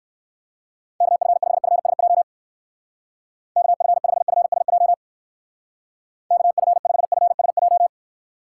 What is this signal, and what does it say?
Morse code, heard as a keyed tone, about 57 wpm, 700 Hz. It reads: KF5FSJ KF5FSJ KF5FSJ